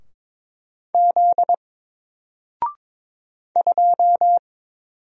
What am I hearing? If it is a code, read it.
Morse code, heard as a keyed tone, about 22 wpm, 700 Hz. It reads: Z 2
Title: Single Letter-Number Alt 22wpm